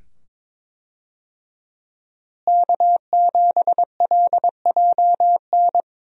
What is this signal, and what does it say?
Morse code, heard as a keyed tone, about 21 wpm, 700 Hz. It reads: K7LJN
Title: USA Call Signs - 5 Characters - Repeated 2x - 22wpm